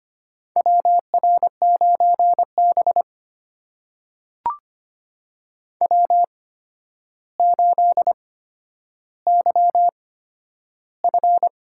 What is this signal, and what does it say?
Morse code, heard as a keyed tone, about 25 wpm, spaced as for 11 wpm, 700 Hz. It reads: WR9B W8YF
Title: USA Call Signs - 4 Characters - 8x Character Spacing - 25wpm